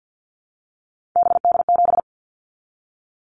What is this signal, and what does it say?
Morse code, heard as a keyed tone, about 50 wpm, 700 Hz. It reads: BD7
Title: Callsign Prefixes for Top 100 DXCC - 50wpm